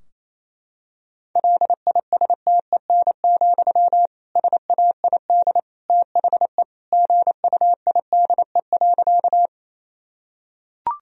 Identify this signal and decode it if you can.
Morse code, heard as a keyed tone, about 28 wpm, 700 Hz. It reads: LISTEN, SAID THE GUIDE.
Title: Sentences from Top 1000 Words - Review Part 3 - 28wpm